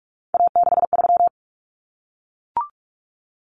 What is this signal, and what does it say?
Morse code, heard as a keyed tone, about 44 wpm, 700 Hz. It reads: A63
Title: Common Cut Numbers - 3-Digit 45wpm